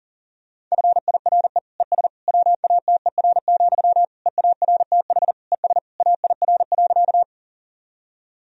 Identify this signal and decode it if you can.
Morse code, heard as a keyed tone, about 40 wpm, 700 Hz. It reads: FIRE ES WATER, EARTH ES AIR.
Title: Sentences from Top 400 Words and Abbreviations 40wpm